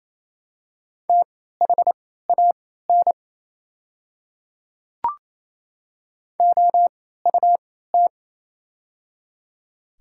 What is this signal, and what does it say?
Morse code, heard as a keyed tone, about 28 wpm, spaced as for 16 wpm, 700 Hz. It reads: THAN OUT
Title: 3x Character Spacing - Top 100 Words - 28wpm